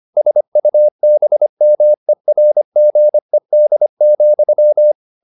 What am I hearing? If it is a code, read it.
Morse code, heard as a keyed tone, about 25 wpm, 600 Hz. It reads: SUBMERGED,
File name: the old man and the sea part 1 25wpm